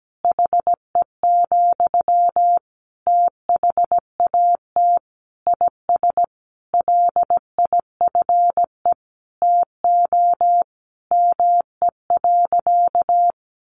Morse code, heard as a keyed tone, about 17 wpm, 700 Hz. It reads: HE, THAT IS LIFE TO ME.
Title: Sentences from Top 500 Words 17wpm